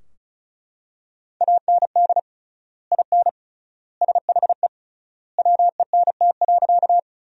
Morse code, heard as a keyed tone, about 34 wpm, 700 Hz. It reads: AND IN SHE WENT.
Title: 3x Word-Spacing - Sentences from Top 200 Words - 35wpm